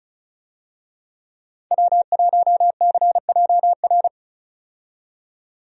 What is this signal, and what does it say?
Morse code, heard as a keyed tone, about 36 wpm, 700 Hz. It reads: W1CJR
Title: USA Call Signs - 5 Characters - 35wpm